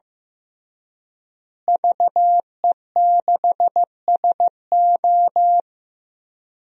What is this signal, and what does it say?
Morse code, heard as a keyed tone, about 15 wpm, 700 Hz. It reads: VE6SO
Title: DXCC-100 Call Signs - 5 Characters - Repeated 3x - 15wpm